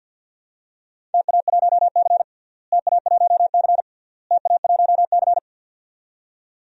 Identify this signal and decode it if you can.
Morse code, heard as a keyed tone, about 51 wpm, 700 Hz. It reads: TA1C TA1C TA1C
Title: DXCC-100 Call Signs - 4 Characters - Repeated 3x - 50wpm